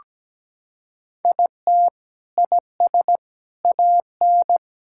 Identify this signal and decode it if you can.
Morse code, heard as a keyed tone, about 17 wpm, 700 Hz. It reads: IT IS AN
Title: Sentences from Top 1000 Words - Review Part 6 - 17wpm